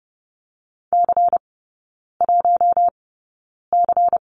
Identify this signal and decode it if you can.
Morse code, heard as a keyed tone, about 30 wpm, spaced as for 9 wpm, 700 Hz. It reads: C1C
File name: Lesson 17e - Letter u - Three Characters 30wpm